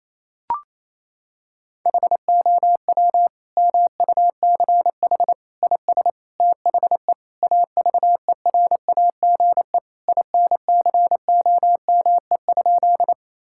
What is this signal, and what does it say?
Morse code, heard as a keyed tone, about 27 wpm, 700 Hz. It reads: HOW MUCH IS THE AVERAGE INCOME?
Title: Sentences from Top 1000 Words - Review Part 6 - 28wpm